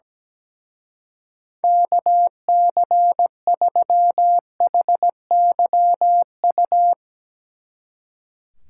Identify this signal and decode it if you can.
Morse code, heard as a keyed tone, about 17 wpm, 700 Hz. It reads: KC3HYU
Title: Speed Racing - USA 2x3 Call Signs 17wpm